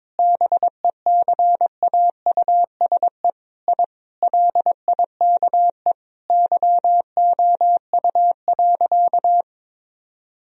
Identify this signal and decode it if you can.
Morse code, heard as a keyed tone, about 22 wpm, 700 Hz. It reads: BECAUSE I LIKE YOU.
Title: Sentences from Top 1000 Words - Review Part 3 - 22wpm